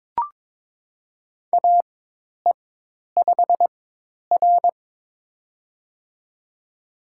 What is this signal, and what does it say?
Morse code, heard as a keyed tone, about 22 wpm, spaced as for 10 wpm, 700 Hz. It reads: AE5R
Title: USA Call Signs - 4 Characters - 4x Character Spacing - 22wpm